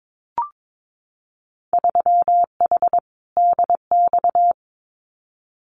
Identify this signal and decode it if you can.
Morse code, heard as a keyed tone, about 22 wpm, 700 Hz. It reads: 3H DX
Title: Winter Field Day Exchanges 22wpm